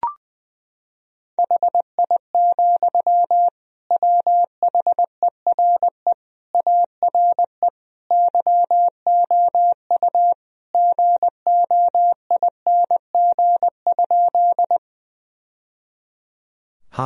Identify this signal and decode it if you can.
Morse code, heard as a keyed tone, about 20 wpm, 700 Hz. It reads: HI, WHERE ARE YOU GOING?
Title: Sentences from Top 1000 Words - Review Part 3 - 20wpm